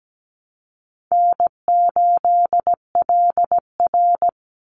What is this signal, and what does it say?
Morse code, heard as a keyed tone, about 17 wpm, 700 Hz. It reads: N8LR
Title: Easy Call Signs - 17wpm